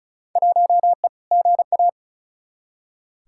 Morse code, heard as a keyed tone, about 36 wpm, 700 Hz. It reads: 1E GA
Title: Field Day Exchanges - 35wpm